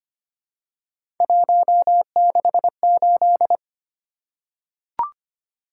Morse code, heard as a keyed tone, about 25 wpm, 700 Hz. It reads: 168
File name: Three Letter-Number Combinations - 25wpm